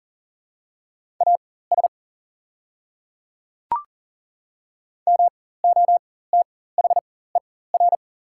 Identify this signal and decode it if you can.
Morse code, heard as a keyed tone, about 40 wpm, spaced as for 19 wpm, 690 Hz. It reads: AS MOTHER
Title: 4x Character Spacing - Top 200 Words - 40wpm